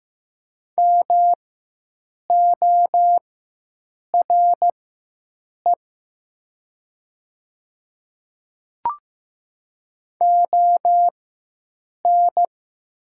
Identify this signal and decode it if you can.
Morse code, heard as a keyed tone, about 15 wpm, spaced as for 7 wpm, 700 Hz. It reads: MORE ON
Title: 4x Character Spacing - Top 100 Words - 15wpm